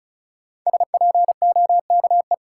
Morse code, heard as a keyed tone, about 36 wpm, 700 Hz. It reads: SPOKE
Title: Sentences from Top 1200 Words - New Word Drill - 35wpm